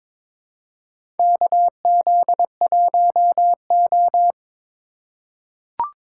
Morse code, heard as a keyed tone, about 22 wpm, 700 Hz. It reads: KZ1O
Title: USA Call Signs - 4 Characters - 3x Character Spacing - 22wpm